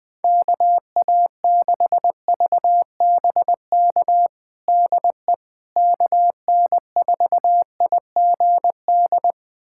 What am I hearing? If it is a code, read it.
Morse code, heard as a keyed tone, about 20 wpm, 700 Hz. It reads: KA6VBK DE KN4IGD
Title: USA Callsign DE Callsign 20wpm